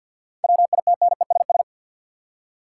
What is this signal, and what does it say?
Morse code, heard as a keyed tone, about 50 wpm, 700 Hz. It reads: WITNESS
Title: Seven Letter Words - 50wpm